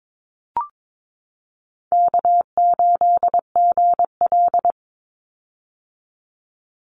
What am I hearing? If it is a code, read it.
Morse code, heard as a keyed tone, about 22 wpm, 700 Hz. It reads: K8GL
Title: Call Signs - 22wpm